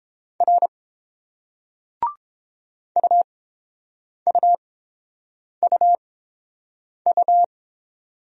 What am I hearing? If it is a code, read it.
Morse code, heard as a keyed tone, about 31 wpm, spaced as for 7 wpm, 700 Hz. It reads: R UUUU